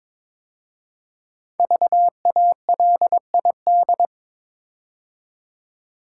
Morse code, heard as a keyed tone, about 22 wpm, 700 Hz. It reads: VALID
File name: 5 Letter Words - First 3 Letters Given - 22wpm